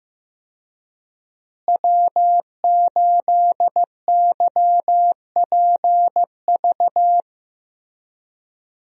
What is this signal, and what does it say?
Morse code, heard as a keyed tone, about 15 wpm, 700 Hz. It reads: W8YPV
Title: USA Call Signs - 5 Characters - Repeated 2x - 15wpm